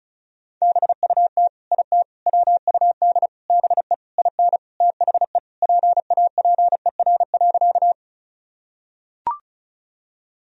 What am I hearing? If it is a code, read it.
Morse code, heard as a keyed tone, about 36 wpm, 700 Hz. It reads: BUT IT WUD BE IN THE PAPER.
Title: Sentences from Top 300 Words and Abbreviations 35wpm